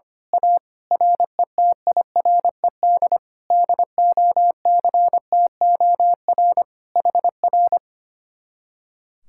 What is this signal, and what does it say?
Morse code, heard as a keyed tone, about 24 wpm, 700 Hz. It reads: A RETIRED DOCTOR HR
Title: Speed-Racing Common QSO Phrases - 25wpm